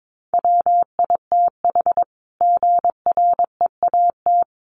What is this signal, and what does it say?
Morse code, heard as a keyed tone, about 22 wpm, 700 Hz. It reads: WITH GREAT
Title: Sentences from Top 1200 Words 22wpm